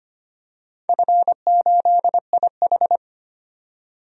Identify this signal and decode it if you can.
Morse code, heard as a keyed tone, about 25 wpm, 700 Hz. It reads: F8IH
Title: DXCC-100 Call Signs - 4 Characters - 25wpm